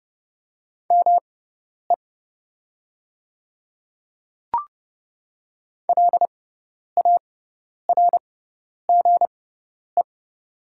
Morse code, heard as a keyed tone, about 31 wpm, spaced as for 10 wpm, 700 Hz. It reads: ME LARGE
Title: 6x Character Spacing - Top 300 Words - 30wpm